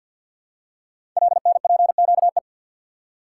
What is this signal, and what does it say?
Morse code, heard as a keyed tone, about 48 wpm, 690 Hz. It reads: LNPQE
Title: Code Groups - 5 Letters - 50wpm